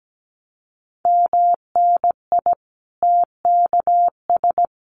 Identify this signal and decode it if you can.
Morse code, heard as a keyed tone, about 17 wpm, 700 Hz. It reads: MNI TKS
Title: Common QSO Phrases 17wpm